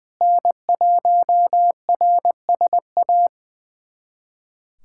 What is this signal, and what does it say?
Morse code, heard as a keyed tone, about 20 wpm, 700 Hz. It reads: N1RSA
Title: USA Callsign DE Callsign - 5 Characters - 20wpm